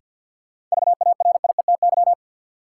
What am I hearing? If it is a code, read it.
Morse code, heard as a keyed tone, about 53 wpm, 700 Hz. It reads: VARIETY
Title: Sets of 3 Words - New Word Drill - 50wpm